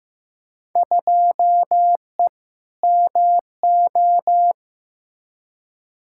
Morse code, heard as a keyed tone, about 15 wpm, 700 Hz. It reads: 2E MO
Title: Field Day Exchanges - 15wpm